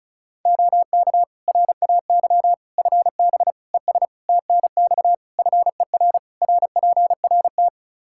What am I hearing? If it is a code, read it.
Morse code, heard as a keyed tone, about 36 wpm, 700 Hz. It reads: OK RAY FB ES TNX FER RPRT
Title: Practice QSO - Exchange 2 - Info - 35wpm